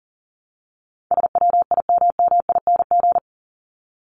Morse code, heard as a keyed tone, about 40 wpm, 700 Hz. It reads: SWIMMING
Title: Top 1501-1750 Words 40wpm